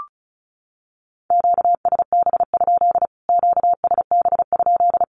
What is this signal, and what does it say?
Morse code, heard as a keyed tone, about 34 wpm, 700 Hz. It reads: QSB? QSB?
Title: Q-Codes Repeated 3x 35wpm